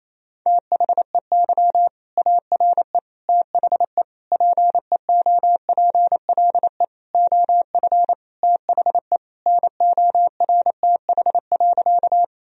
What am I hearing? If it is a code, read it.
Morse code, heard as a keyed tone, about 28 wpm, 700 Hz. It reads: THEY ARE THE PEOPLE OF THE NORTH.
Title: Sentences from Top 1000 Words - Review Part 1 - 28wpm